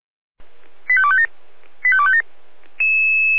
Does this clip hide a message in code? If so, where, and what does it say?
Yes: five-tone selective call (ZVEI) from 0.9 to 2.2 s, heard as quick runs of five tones, tones of about 70 ms. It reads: 85257 85257